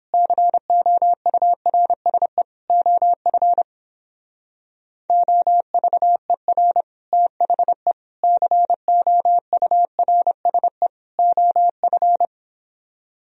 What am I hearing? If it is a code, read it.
Morse code, heard as a keyed tone, about 30 wpm, 700 Hz. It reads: COURSE OF OVER THE COURSE OF